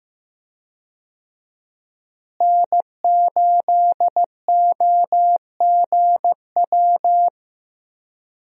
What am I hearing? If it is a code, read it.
Morse code, heard as a keyed tone, about 15 wpm, 700 Hz. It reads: N8OGW
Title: USA Call Signs - 5 Characters - Repeated 2x - 15wpm